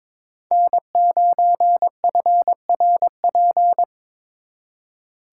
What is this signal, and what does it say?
Morse code, heard as a keyed tone, about 26 wpm, 700 Hz. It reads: N9FRP
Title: Speed Racing - USA 1x3 Call Signs 22wpm